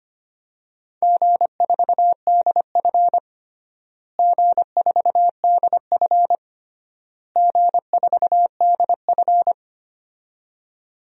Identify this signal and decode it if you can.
Morse code, heard as a keyed tone, about 25 wpm, 700 Hz. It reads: G4DF G4DF G4DF
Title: DXCC-100 Call Signs - 4 Characters - Repeated 3x - 25wpm